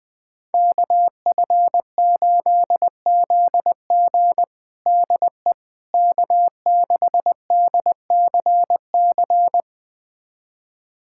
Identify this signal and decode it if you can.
Morse code, heard as a keyed tone, about 21 wpm, 700 Hz. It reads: KF8ZG DE K6DCC
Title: USA Callsign DE Callsign - 5 Characters - 20wpm